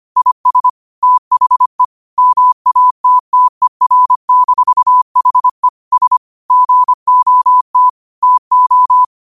Morse code, heard as a keyed tone, about 25 wpm, 1000 Hz. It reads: IS THE MATTER-HE S GOT TO